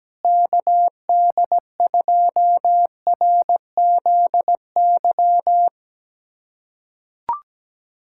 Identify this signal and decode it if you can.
Morse code, heard as a keyed tone, about 17 wpm, 700 Hz. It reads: KD2RZY
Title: USA Callsign DE Callsign - 6 Characters - 17wpm